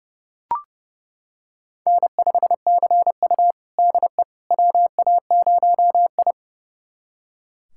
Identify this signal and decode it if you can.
Morse code, heard as a keyed tone, about 30 wpm, 700 Hz. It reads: N5CU DE WA0I
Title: USA Callsign DE Callsign - 4 Characters - 30wpm